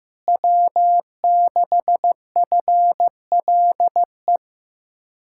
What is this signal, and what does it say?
Morse code, heard as a keyed tone, about 15 wpm, 700 Hz. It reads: W6FLE
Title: Speed-Racing Common QSO Phrases Encore - 15wpm